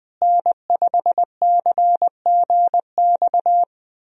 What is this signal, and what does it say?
Morse code, heard as a keyed tone, about 20 wpm, 700 Hz. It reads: N5CGX
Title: USA Call Signs - 5 Characters - Repeated 3x - 20wpm